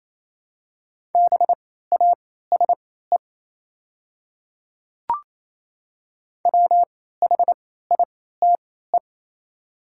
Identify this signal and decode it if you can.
Morse code, heard as a keyed tone, about 28 wpm, spaced as for 16 wpm, 700 Hz. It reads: BASE WHITE
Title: 3x Character Spacing - Top 500 Words - 28wpm